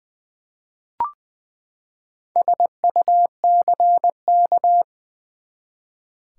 Morse code, heard as a keyed tone, about 20 wpm, 700 Hz. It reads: SUCK
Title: Four Letter Words - 20wpm